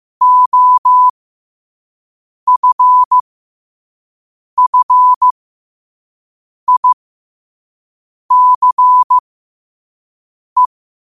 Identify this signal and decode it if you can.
Morse code, heard as a keyed tone, about 15 wpm, spaced as for 5 wpm, 1000 Hz. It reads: OFFICE
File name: qotdfiles / 5wpm